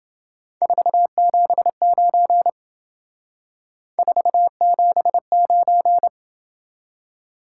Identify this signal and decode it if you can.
Morse code, heard as a keyed tone, about 28 wpm, 700 Hz. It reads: 479 479